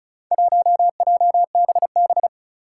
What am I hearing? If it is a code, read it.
Morse code, heard as a keyed tone, about 36 wpm, 700 Hz. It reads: 1JBB
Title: Four Challenging Letter-Number Combinations 35wpm